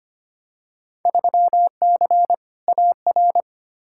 Morse code, heard as a keyed tone, about 25 wpm, 700 Hz. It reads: 3C AR
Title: Field Day Exchanges Encore - 25wpm